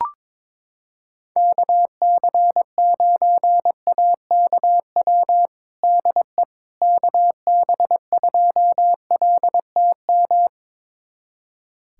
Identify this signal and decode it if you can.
Morse code, heard as a keyed tone, about 22 wpm, 700 Hz. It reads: KC9AKW DE KB2LTM
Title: USA Callsign DE Callsign 22wpm